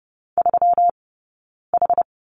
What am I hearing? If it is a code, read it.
Morse code, heard as a keyed tone, about 30 wpm, spaced as for 9 wpm, 700 Hz. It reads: 3H
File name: Lesson 20d - Number 3 - Two Characters 30wpm